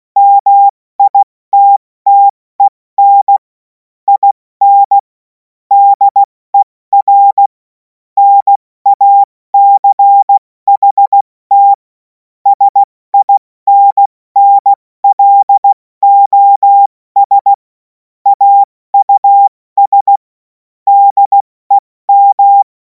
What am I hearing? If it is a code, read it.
Morse code, heard as a keyed tone, about 16 wpm, 800 Hz. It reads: MITTEN IN DER NACHT SINNLOS AUS DEM
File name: Morsekurs / Witze / Witz 07. Juli 2026 70 F80